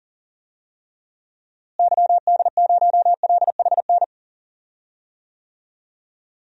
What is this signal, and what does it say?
Morse code, heard as a keyed tone, about 37 wpm, 700 Hz. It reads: YD0LHN
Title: DXCC-100 Call Signs - 6 Characters - Repeated 2x - 40wpm